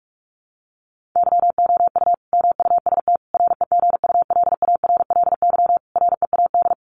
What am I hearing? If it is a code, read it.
Morse code, heard as a keyed tone, about 44 wpm, 700 Hz. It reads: YOU MUST REGULARLY READ
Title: Sentences from Top 1400 Words 45wpm